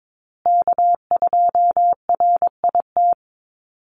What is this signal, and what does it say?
Morse code, heard as a keyed tone, about 22 wpm, 700 Hz. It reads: K2RIT
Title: USA Call Signs Repeated 2x 22wpm